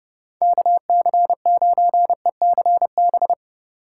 Morse code, heard as a keyed tone, about 27 wpm, 700 Hz. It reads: KC9ECB